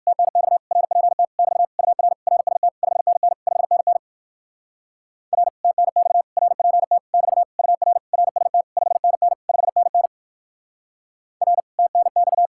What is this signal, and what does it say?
Morse code, heard as a keyed tone, about 58 wpm, 700 Hz. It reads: TNX RPT <BT> UR RST 5NN 5NN R TNX RPT <BT> UR RST 5NN 5NN R TNX